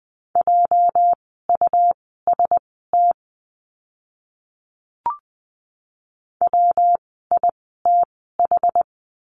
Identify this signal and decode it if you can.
Morse code, heard as a keyed tone, about 20 wpm, spaced as for 14 wpm, 700 Hz. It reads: JUST WITH
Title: 230 Lesson 37f - Letter x - Words 20wpm